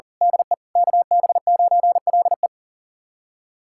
Morse code, heard as a keyed tone, about 40 wpm, 700 Hz. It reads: DE KB9LE
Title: USA Callsign DE Callsign - 5 Characters - 40wpm